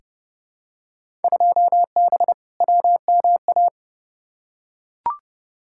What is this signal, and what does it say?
Morse code, heard as a keyed tone, about 29 wpm, 700 Hz. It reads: 2B WMA
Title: Field Day Exchanges Encore - 30wpm